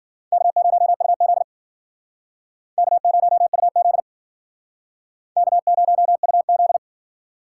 Morse code, heard as a keyed tone, about 50 wpm, 700 Hz. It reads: K0UZ K0UZ K0UZ